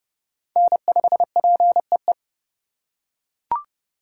Morse code, heard as a keyed tone, about 30 wpm, 700 Hz. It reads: N5PEE